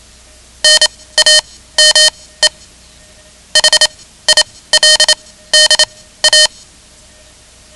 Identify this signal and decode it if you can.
Morse code, heard as a keyed tone, about 27 wpm, spaced as for 16 wpm, 3050 Hz. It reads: NAME HILDA
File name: ham radio / CodeQuick / Code Quick 10